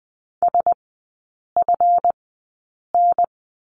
Morse code, heard as a keyed tone, about 20 wpm, spaced as for 8 wpm, 700 Hz. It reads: SFN
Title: 172 Lesson 28e - Number 9 - Three Characters 20wpm